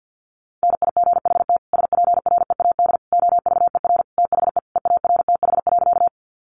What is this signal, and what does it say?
Morse code, heard as a keyed tone, about 50 wpm, 700 Hz. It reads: NIGHT SPREAD OVER THE EARTH.
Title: Sentences from Top 1200 Words 50wpm